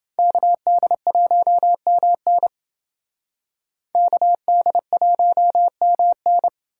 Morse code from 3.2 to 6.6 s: KD1MN

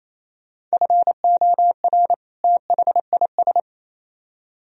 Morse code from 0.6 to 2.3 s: FOR

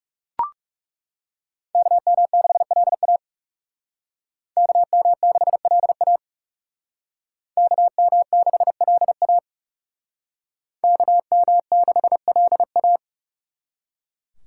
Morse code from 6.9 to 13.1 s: KM6LA KM6LA

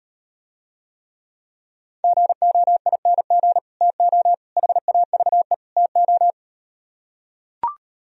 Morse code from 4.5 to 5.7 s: HAVE

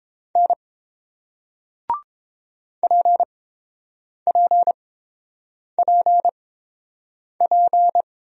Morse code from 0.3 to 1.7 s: N

